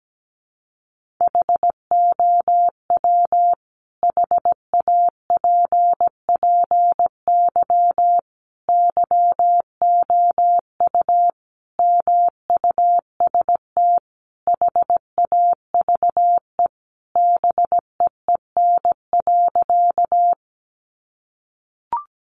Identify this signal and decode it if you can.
Morse code, heard as a keyed tone, about 17 wpm, 700 Hz. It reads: HOW HAPPY YOU MUST HAVE BEEN.
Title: Sentences from Top 1000 Words 17wpm